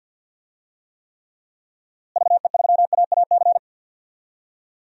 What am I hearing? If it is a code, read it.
Morse code, heard as a keyed tone, about 52 wpm, 700 Hz. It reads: VE3AAC
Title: DXCC-100 Call Signs - 6 Characters - Repeated 3x - 50wpm